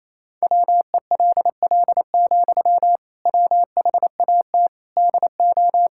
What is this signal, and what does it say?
Morse code, heard as a keyed tone, about 28 wpm, 700 Hz. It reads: WELL, WHAT DO
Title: Sentences from Top 1600 Words 28wpm